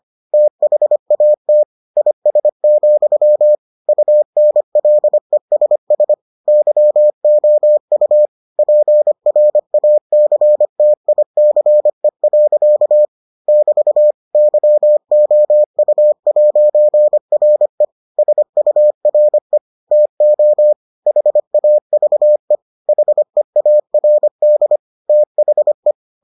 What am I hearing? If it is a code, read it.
Morse code, heard as a keyed tone, about 25 wpm, 600 Hz. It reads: THAT IS, UNLESS YOU PRACTICE. <BT> YOU'RE SURE TO HAVE HEARD THE